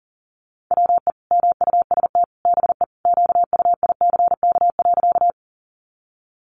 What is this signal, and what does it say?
Morse code, heard as a keyed tone, about 40 wpm, 700 Hz. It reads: WE MUST BE QUICK.
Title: Sentences from Top 500 Words 40wpm